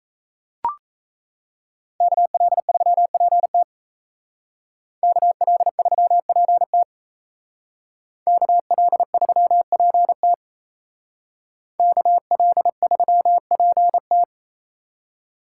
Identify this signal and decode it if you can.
Morse code, heard as a keyed tone, about 33 wpm, 700 Hz. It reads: KL3PT KL3PT KL3PT KL3PT
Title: Speed Racing - USA 2x2 Call Signs 28wpm